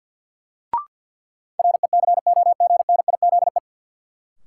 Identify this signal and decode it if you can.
Morse code, heard as a keyed tone, about 53 wpm, 700 Hz. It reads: RECOGNIZE
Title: Sentences from Top 1400 Words - New Word Drill - 50wpm